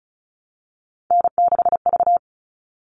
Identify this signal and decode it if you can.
Morse code, heard as a keyed tone, about 34 wpm, 700 Hz. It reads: N6V
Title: All Cut Numbers - 3-Digit 35wpm